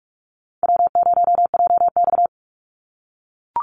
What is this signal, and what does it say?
Morse code, heard as a keyed tone, about 44 wpm, 700 Hz. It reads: W0JX
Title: Call Signs - 45wpm